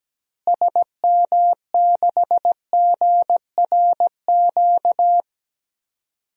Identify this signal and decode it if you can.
Morse code, heard as a keyed tone, about 17 wpm, 700 Hz. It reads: SM6GRQ
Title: DXCC-100 Call Signs - 6 Characters - Repeated 2x - 17wpm